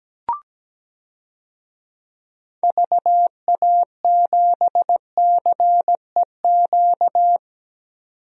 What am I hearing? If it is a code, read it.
Morse code, heard as a keyed tone, about 17 wpm, 700 Hz. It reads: VA7CEQ
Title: DXCC-100 Call Signs - 6 Characters - 17wpm